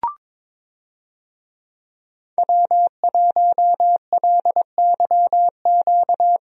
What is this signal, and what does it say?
Morse code, heard as a keyed tone, about 22 wpm, 700 Hz. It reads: W1LYQ
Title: USA Call Signs - 5 Characters - 22wpm